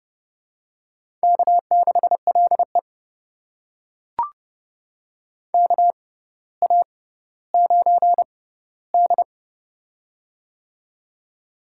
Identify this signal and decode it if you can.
Morse code, heard as a keyed tone, about 30 wpm, spaced as for 15 wpm, 700 Hz. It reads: K6LE KA9D
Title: USA Call Signs - 4 Characters - 6x Character Spacing - 30wpm